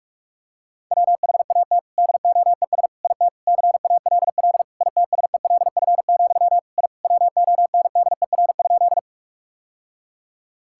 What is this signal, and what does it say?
Morse code, heard as a keyed tone, about 42 wpm, 700 Hz. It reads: WHAT DOES IT CALL ITSELF, I WONDER?
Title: Sentences from Top 1000 Words - Review Part 5 - 45wpm